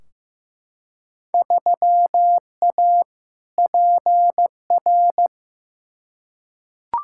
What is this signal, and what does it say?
Morse code, heard as a keyed tone, about 15 wpm, 700 Hz. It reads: 3A PR
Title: Field Day Exchanges Encore - 15wpm